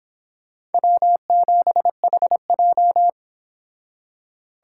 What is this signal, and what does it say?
Morse code, heard as a keyed tone, about 31 wpm, 700 Hz. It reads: W7HJ